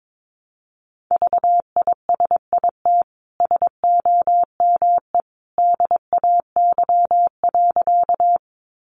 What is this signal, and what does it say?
Morse code, heard as a keyed tone, about 22 wpm, 700 Hz. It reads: VISIT SOME DAY.
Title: Sentences from Top 900 Words 22wpm